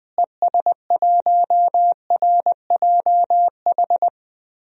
Morse code, heard as a keyed tone, about 21 wpm, 700 Hz. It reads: ES1RJH
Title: DXCC-100 Call Signs - 6 Characters - Repeated 2x - 20wpm